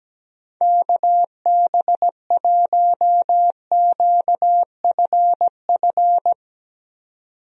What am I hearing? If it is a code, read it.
Morse code, heard as a keyed tone, about 17 wpm, 700 Hz. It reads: KB1QFF
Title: USA Call Signs - 6 Characters - 17wpm